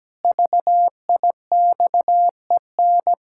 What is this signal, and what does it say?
Morse code, heard as a keyed tone, about 17 wpm, 700 Hz. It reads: VIXEN
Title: Pangram Challenge v2 - 17wpm